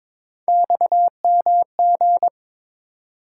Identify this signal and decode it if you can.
Morse code, heard as a keyed tone, about 22 wpm, 700 Hz. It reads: XMG
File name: practice / 22wpm / Three Letter-Number Combinations - 22wpm